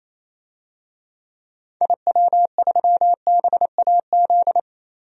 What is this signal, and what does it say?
Morse code, heard as a keyed tone, about 28 wpm, 700 Hz. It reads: IW3BAZ
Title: DXCC-100 Call Signs - 6 Characters - 28wpm